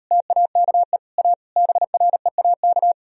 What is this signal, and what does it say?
Morse code, heard as a keyed tone, about 39 wpm, 700 Hz. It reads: TAKE A BREAK